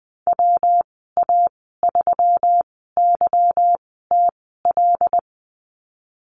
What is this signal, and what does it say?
Morse code, heard as a keyed tone, about 20 wpm, spaced as for 15 wpm, 700 Hz. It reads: WA3YTL
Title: 154 Lesson 25g - Letter p - Call Signs 20wpm